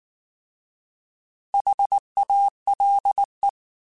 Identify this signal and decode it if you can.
Morse code, heard as a keyed tone, about 19 wpm, 800 Hz. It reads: HALE